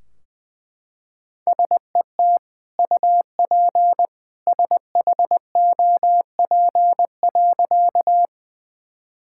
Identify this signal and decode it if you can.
Morse code, heard as a keyed tone, about 20 wpm, 700 Hz. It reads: SET UP SHOP.
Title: Sentences from Top 1600 Words 20wpm